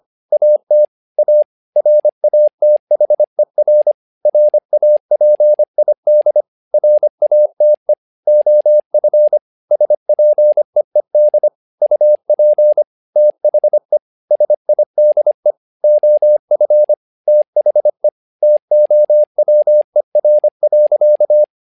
Morse code, heard as a keyed tone, about 25 wpm, 600 Hz. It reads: AT A RATHER RAPID RATE OF SPEED UP THE SIDE OF THE TOWER.
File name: learning center / tower trouble / tower trouble 25wpm